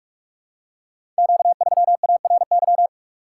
Morse code, heard as a keyed tone, about 44 wpm, 700 Hz. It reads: Q3ARY
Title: Code Groups - 5 Letter:Numbers - 45wpm